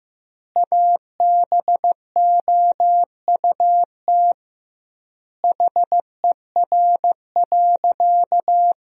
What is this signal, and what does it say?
Morse code, heard as a keyed tone, about 15 wpm, 700 Hz. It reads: ABOUT HER.
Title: 2x Word-Spacing - Sentences from Top 200 Words - 15wpm